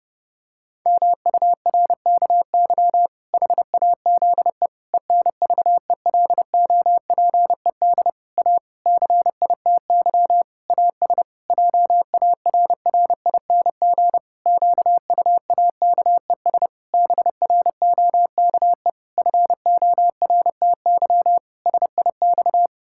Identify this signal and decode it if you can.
Morse code, heard as a keyed tone, about 30 wpm, 700 Hz. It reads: MURKY HAZE ENVELOPED A CITY AS JARRING QUAKES BROKE FORTY SIX
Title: Pangram Challenge v2 - 30wpm